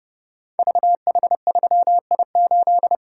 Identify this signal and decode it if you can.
Morse code, heard as a keyed tone, about 30 wpm, 700 Hz. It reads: VH3I8
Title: Code Groups - 5 Letter:Numbers - 30wpm